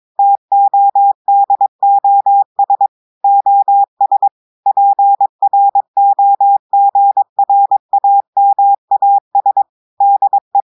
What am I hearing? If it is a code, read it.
Morse code, heard as a keyed tone, about 22 wpm, 800 Hz. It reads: TODOS OS PROGRAMAS DE
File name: audios / portugais / portugais texte en clair 2 vitesse 22 mots minutes